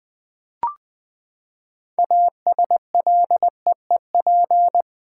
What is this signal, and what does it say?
Morse code, heard as a keyed tone, about 20 wpm, 700 Hz. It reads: ASLEEP
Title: Six Letter Words - 20wpm